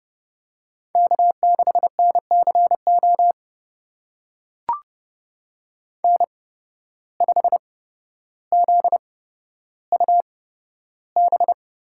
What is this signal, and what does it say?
Morse code, heard as a keyed tone, about 32 wpm, spaced as for 13 wpm, 700 Hz. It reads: K6NCO N5ZUB
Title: USA Call Signs - 5 Characters - 8x Character Spacing - 30wpm